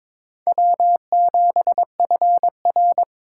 Morse code, heard as a keyed tone, about 26 wpm, 700 Hz. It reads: W7FR
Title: Speed Racing - USA Call Signs 22wpm